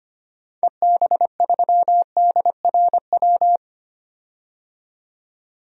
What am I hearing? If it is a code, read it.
Morse code, heard as a keyed tone, about 25 wpm, 700 Hz. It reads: EB3DRW
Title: DXCC-100 Call Signs - 6 Characters - 25wpm